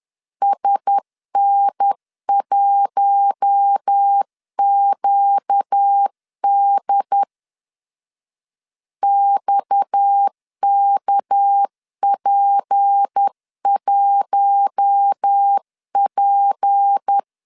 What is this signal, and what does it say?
Morse code, heard as a keyed tone, about 10 wpm, 800 Hz. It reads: SN1QD XKP1P
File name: LEZIONI / 7 GF9Z / 7 GF 10wpm 2514